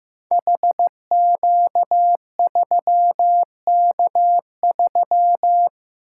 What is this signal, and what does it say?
Morse code, heard as a keyed tone, about 15 wpm, 700 Hz. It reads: HQ3K3
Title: Code Groups - 5 Letter:Numbers - 15wpm